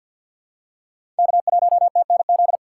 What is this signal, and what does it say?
Morse code, heard as a keyed tone, about 53 wpm, 700 Hz. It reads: K1TNZ
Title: USA Call Signs - 6x Character Spacing - 50wpm